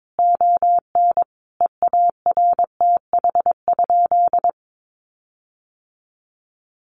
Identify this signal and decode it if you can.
Morse code, heard as a keyed tone, about 22 wpm, 700 Hz. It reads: ON EARTH?
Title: Sentences from Top 1500 Words 22wpm